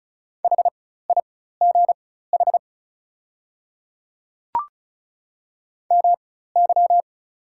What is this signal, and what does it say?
Morse code, heard as a keyed tone, about 36 wpm, spaced as for 16 wpm, 700 Hz. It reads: HIGH MY
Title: 4x Character Spacing - Top 200 Words - 35wpm